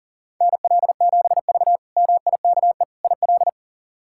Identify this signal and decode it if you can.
Morse code, heard as a keyed tone, about 40 wpm, 700 Hz. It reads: NL7V MIKE IL
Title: K1USN SST QSO Practice - 40wpm